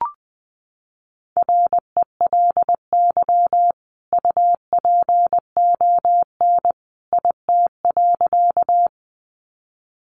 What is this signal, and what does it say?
Morse code, heard as a keyed tone, about 20 wpm, 700 Hz. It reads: RELY UPON IT.
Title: Sentences from Top 1200 Words 20wpm